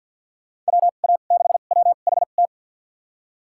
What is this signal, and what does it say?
Morse code, heard as a keyed tone, about 47 wpm, spaced as for 35 wpm, 700 Hz. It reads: WA6WHT
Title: USA Call Signs - 2x Character Spacing - 50wpm